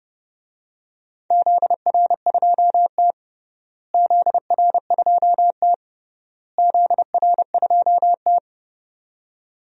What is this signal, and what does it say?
Morse code, heard as a keyed tone, about 30 wpm, 700 Hz. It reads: ZR2T ZR2T ZR2T
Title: DXCC-100 Call Signs - 4 Characters - Repeated 3x - 30wpm